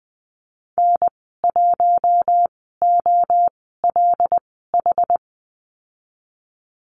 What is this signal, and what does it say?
Morse code, heard as a keyed tone, about 19 wpm, spaced as for 14 wpm, 700 Hz. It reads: N1OLH
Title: 076 Lesson 13g - Letter L - Call Signs 20wpm